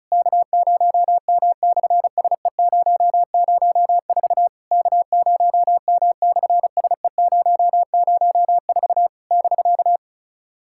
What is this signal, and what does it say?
Morse code, heard as a keyed tone, about 36 wpm, 700 Hz. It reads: K0M/SE004 K0M/SE004 <BK>
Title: SOTA QSO Practice 35wpm